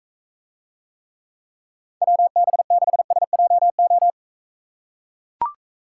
Morse code, heard as a keyed tone, about 43 wpm, 700 Hz. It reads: WB6SJO